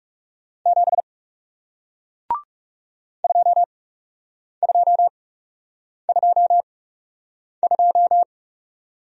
Morse code, heard as a keyed tone, about 38 wpm, spaced as for 8 wpm, 700 Hz. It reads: 7 2222